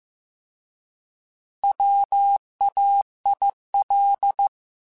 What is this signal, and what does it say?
Morse code, heard as a keyed tone, about 15 wpm, 800 Hz. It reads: WAIL
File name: Alphabetical / Alphabetical mp3s / Ws @13